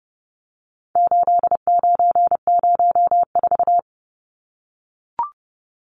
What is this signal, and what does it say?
Morse code, heard as a keyed tone, about 30 wpm, 700 Hz. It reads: 8904
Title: Numbers - 4-Digit 30wpm